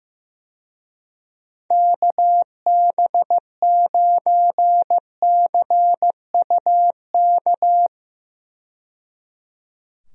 Morse code, heard as a keyed tone, about 15 wpm, 700 Hz. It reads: KB9CUK
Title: USA Call Signs - 6 Characters - 15wpm